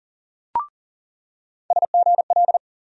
Morse code, heard as a keyed tone, about 40 wpm, 700 Hz. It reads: SGL
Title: USA Call Sign Postfixes - 3 Letters 40wpm